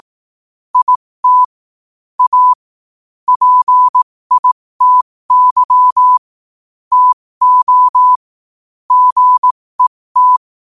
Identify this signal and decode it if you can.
Morse code, heard as a keyed tone, about 18 wpm, spaced as for 15 wpm, 1000 Hz. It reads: IT A PITY TO GET
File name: qotdfiles / 15wpm